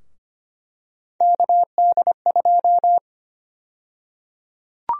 Morse code, heard as a keyed tone, about 25 wpm, 700 Hz. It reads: KD2
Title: USA Call Sign Prefixes - 25wpm